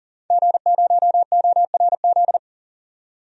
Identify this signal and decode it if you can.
Morse code, heard as a keyed tone, about 40 wpm, 700 Hz. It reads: G0ORZ
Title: DXCC-100 Call Signs - 5 Characters - 40wpm